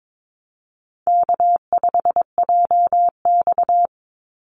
Morse code, heard as a keyed tone, about 22 wpm, 700 Hz. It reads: K5JX
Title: Call Signs Repeated 2x - 22wpm